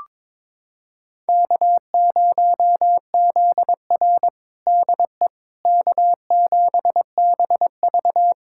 Morse code, heard as a keyed tone, about 22 wpm, 700 Hz. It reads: K0ZR DE K7BV
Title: Practice QSO - Exchange 4 - Ending - 22wpm